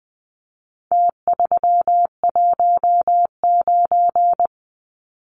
Morse code, heard as a keyed tone, about 20 wpm, 700 Hz. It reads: T319
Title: Numbers - Contest Style - 4-Digit 20wpm